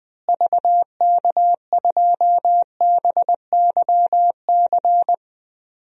Morse code, heard as a keyed tone, about 20 wpm, 700 Hz. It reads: VK2BYC
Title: DXCC-100 Call Signs - 6 Characters - Repeated 3x - 20wpm